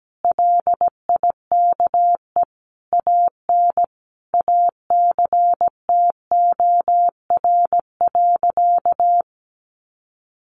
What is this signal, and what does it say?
Morse code, heard as a keyed tone, about 17 wpm, 700 Hz. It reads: LIKE AN ACTOR.
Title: Sentences from Top 1500 Words 17wpm